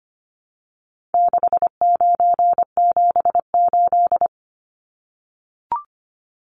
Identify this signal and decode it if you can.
Morse code, heard as a keyed tone, about 25 wpm, 700 Hz. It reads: 6978
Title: Numbers - 4-Digit 25wpm